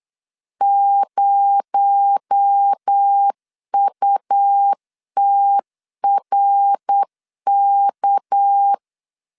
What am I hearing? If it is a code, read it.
Morse code, heard as a keyed tone, about 8 wpm, 800 Hz. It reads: 0UTRK
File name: 8wpm